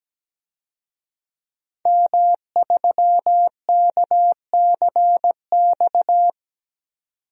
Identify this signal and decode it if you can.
Morse code, heard as a keyed tone, about 17 wpm, 700 Hz. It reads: M3KCX
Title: DXCC-100 Call Signs - 5 Characters - 17wpm